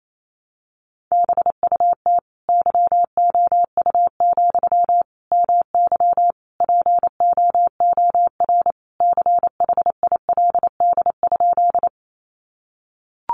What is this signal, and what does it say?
Morse code, heard as a keyed tone, about 28 wpm, 700 Hz. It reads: BUT YOU, MY POOR CHILD?
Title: Sentences from Top 1000 Words 28wpm